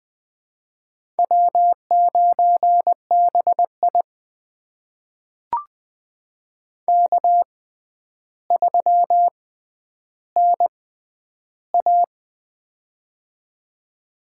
Morse code, heard as a keyed tone, about 21 wpm, spaced as for 10 wpm, 700 Hz. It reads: W9BI K3NA
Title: USA Call Signs - 6x Character Spacing - 20wpm